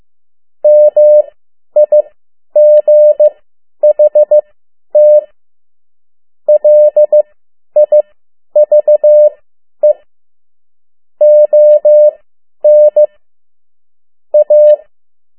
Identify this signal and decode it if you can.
Morse code, heard as a keyed tone, about 15 wpm, spaced as for 10 wpm, 600 Hz. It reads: MIGHT LIVE ON A
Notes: Morse code 10 WPM